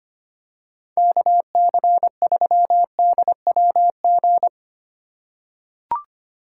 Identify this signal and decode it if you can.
Morse code, heard as a keyed tone, about 25 wpm, 690 Hz. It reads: KC3DWG
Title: USA Call Signs - 6 Characters - 2x Character Spacing - 25wpm